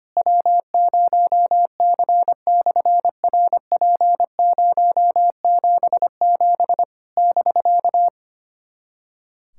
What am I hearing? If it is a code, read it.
Morse code, heard as a keyed tone, about 25 wpm, 700 Hz. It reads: W0C/RP077 <BK>
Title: SOTA QSO Practice 25wpm